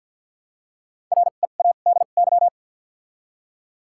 Morse code, heard as a keyed tone, about 53 wpm, 700 Hz. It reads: R E A D Y
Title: 2x Character Spacing - Top 500 Words - 50wpm